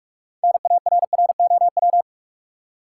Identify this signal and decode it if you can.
Morse code, heard as a keyed tone, about 46 wpm, 700 Hz. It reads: NARROW
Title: Six Letter Words - 45wpm